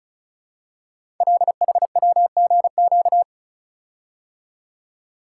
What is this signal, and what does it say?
Morse code, heard as a keyed tone, about 36 wpm, 700 Hz. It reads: LHWGQ
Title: Code Groups - 5 Letters - 35wpm